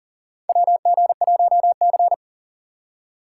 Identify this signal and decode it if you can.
Morse code, heard as a keyed tone, about 40 wpm, 700 Hz. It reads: WG1C
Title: USA Call Signs - 4 Characters - 3x Character Spacing - 40wpm